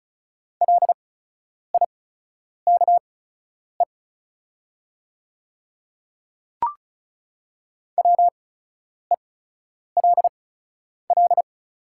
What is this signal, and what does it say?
Morse code, heard as a keyed tone, about 36 wpm, spaced as for 10 wpm, 690 Hz. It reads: LIKE WELL
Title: 8x Character Spacing - Top 300 Words - 35wpm